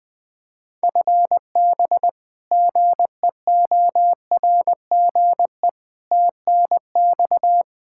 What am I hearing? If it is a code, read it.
Morse code, heard as a keyed tone, about 21 wpm, 700 Hz. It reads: FB GEORGE TNX
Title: K1USN SST QSO Practice 20wpm